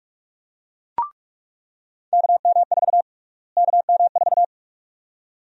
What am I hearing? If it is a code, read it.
Morse code, heard as a keyed tone, about 46 wpm, 700 Hz. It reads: KM4 KM4
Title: USA Call Sign Prefixes - Repeated 2x - 45wpm